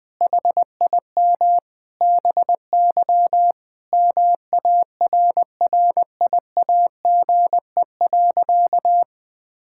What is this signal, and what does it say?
Morse code, heard as a keyed tone, about 20 wpm, 700 Hz. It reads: HIM BY MARRIAGE.
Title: Sentences from Top 1000 Words - Review Part 5 - 20wpm